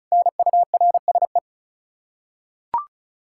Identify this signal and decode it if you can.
Morse code, heard as a keyed tone, about 36 wpm, 700 Hz. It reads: NURSE
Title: Practice QSO - Exchange 3 - More Info - 35wpm